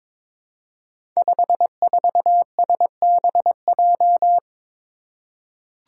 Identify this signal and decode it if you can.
Morse code, heard as a keyed tone, about 22 wpm, 700 Hz. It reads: 54SBJ
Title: Five Challenging Letter-Number Combinations 22wpm